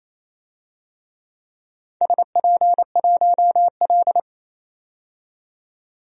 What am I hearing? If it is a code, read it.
Morse code, heard as a keyed tone, about 27 wpm, 690 Hz. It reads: SP1L
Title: DXCC-100 Call Signs - 4 Characters - 28wpm